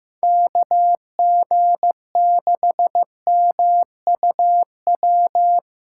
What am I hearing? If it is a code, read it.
Morse code, heard as a keyed tone, about 15 wpm, 700 Hz. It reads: KG6MUW
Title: USA Callsign DE Callsign 15wpm